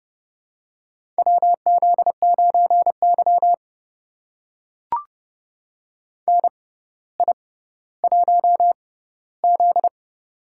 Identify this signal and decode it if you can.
Morse code, heard as a keyed tone, about 30 wpm, spaced as for 15 wpm, 700 Hz. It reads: WZ9Y NI1Z
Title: USA Call Signs - 4 Characters - 6x Character Spacing - 30wpm